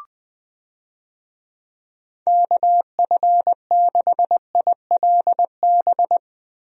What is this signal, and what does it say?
Morse code, heard as a keyed tone, about 20 wpm, 700 Hz. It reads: KF6ILB
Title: USA Call Signs - 6 Characters - Repeated 2x - 20wpm